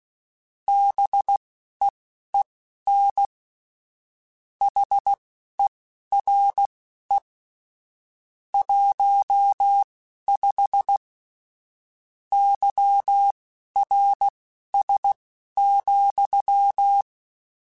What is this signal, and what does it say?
Morse code, heard as a keyed tone, about 16 wpm, spaced as for 11 wpm, 800 Hz. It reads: BEEN HERE 15 YRS,
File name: QSOs / QSOs-mp3s / WD0BGP 1@13